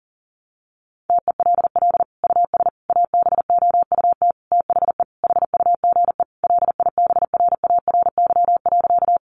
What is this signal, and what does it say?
Morse code, heard as a keyed tone, about 37 wpm, 700 Hz. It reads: TELL US ABOUT THE HUGE LIBRARY.
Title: Sentences from Top 1100 Words 40wpm